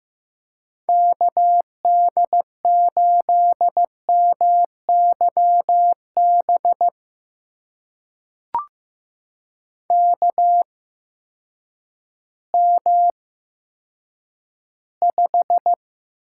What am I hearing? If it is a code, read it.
Morse code, heard as a keyed tone, about 15 wpm, spaced as for 9 wpm, 700 Hz. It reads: KD8MYB KM5
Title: USA Call Signs - 8x Character Spacing - 15wpm